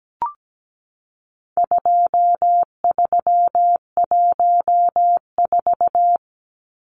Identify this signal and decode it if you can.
Morse code, heard as a keyed tone, about 17 wpm, 700 Hz. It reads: 2314
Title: Numbers - 4-Digit 17wpm